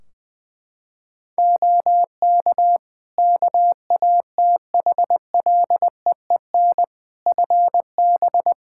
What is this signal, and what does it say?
Morse code, heard as a keyed tone, about 21 wpm, 700 Hz. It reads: OK KATHLEEN FB
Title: Practice QSO - Exchange 2 - Info - 20wpm